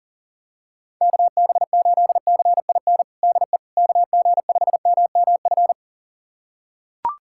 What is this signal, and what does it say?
Morse code, heard as a keyed tone, about 37 wpm, 700 Hz. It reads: KB8CIN DE KG5MMF
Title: USA Callsign DE Callsign - 6 Characters - 40wpm